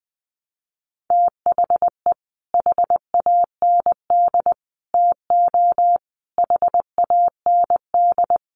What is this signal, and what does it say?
Morse code, heard as a keyed tone, about 20 wpm, 700 Hz. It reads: THE HAND TO HAND
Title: Sentences from Top 1300 Words 20wpm